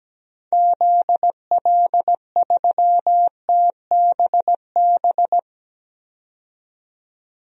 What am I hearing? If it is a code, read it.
Morse code, heard as a keyed tone, about 17 wpm, 700 Hz. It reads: ZL3TBB
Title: DXCC-100 Call Signs - 6 Characters - 17wpm